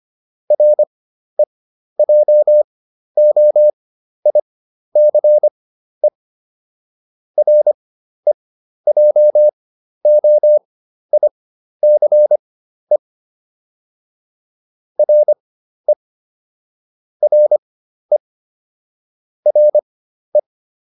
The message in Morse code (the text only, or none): REJOICE REJOICE RE RE RE